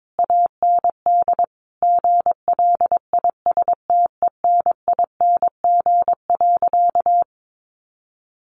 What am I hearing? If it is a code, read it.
Morse code, heard as a keyed tone, about 22 wpm, 700 Hz. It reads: AND GLISTENING.
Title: A Sound Of Thunder 22wpm